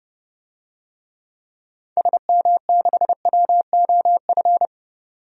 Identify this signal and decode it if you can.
Morse code, heard as a keyed tone, about 32 wpm, 700 Hz. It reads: SM6WOF
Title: DXCC-100 Call Signs - 6 Characters - 30wpm